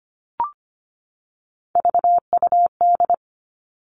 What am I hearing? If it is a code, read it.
Morse code, heard as a keyed tone, about 25 wpm, 700 Hz. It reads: VUD
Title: Three Challenging Letter-Number Combinations 25wpm